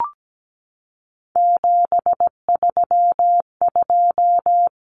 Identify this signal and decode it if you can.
Morse code, heard as a keyed tone, about 17 wpm, 700 Hz. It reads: 732
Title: Numbers - 3-Digit 17wpm